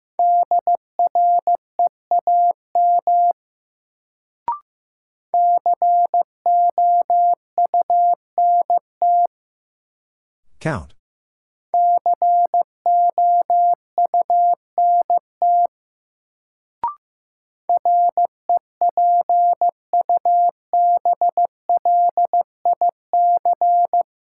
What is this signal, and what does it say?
Morse code, heard as a keyed tone, about 15 wpm, 700 Hz. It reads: DREAM COUNT COUNT REPUBLIC
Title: Sentences from Top 1500 Words - New Word Drill Alt - 15wpm